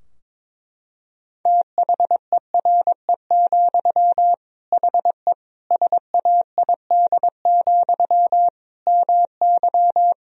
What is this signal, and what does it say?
Morse code, heard as a keyed tone, about 22 wpm, 700 Hz. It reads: THERE, HE SAID, MY
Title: Sentences from Top 1000 Words - Review Part 2 - 22wpm